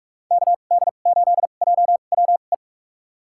Morse code, heard as a keyed tone, about 46 wpm, spaced as for 32 wpm, 700 Hz. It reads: KD8JWE
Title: USA Call Signs - 6 Characters - 2x Character Spacing - 45wpm